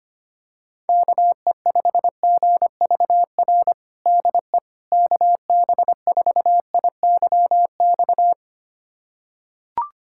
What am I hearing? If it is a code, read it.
Morse code, heard as a keyed tone, about 25 wpm, 700 Hz. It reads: KE5GVR DE KB4IYX
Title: USA Callsign DE Callsign 25wpm